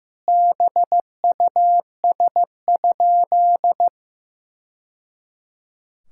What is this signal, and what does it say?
Morse code, heard as a keyed tone, about 16 wpm, 700 Hz. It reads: BUS?
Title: Sentences from Top 1000 Words - Review Part 6 - 15wpm